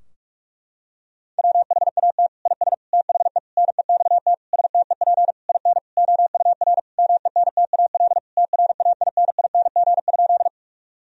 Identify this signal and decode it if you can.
Morse code, heard as a keyed tone, about 42 wpm, 700 Hz. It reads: WHAT IS THE NEXT STEP IN OUR MENTAL TRAINING?
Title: Sentences from Top 1000 Words - Review Part 4 - 45wpm